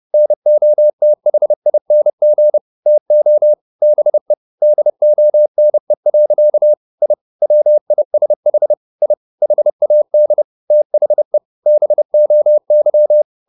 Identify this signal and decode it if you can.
Morse code, heard as a keyed tone, about 30 wpm, 600 Hz. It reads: NOTHING TO BE DONE. I WISH I HAD THE BOY
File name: the old man and the sea / the old man and the sea 1 30wpm